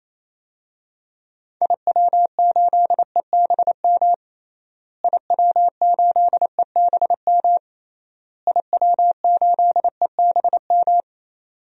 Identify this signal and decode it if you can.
Morse code, heard as a keyed tone, about 28 wpm, 700 Hz. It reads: IW8EBM IW8EBM IW8EBM
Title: DXCC-100 Call Signs - 6 Characters - Repeated 3x - 28wpm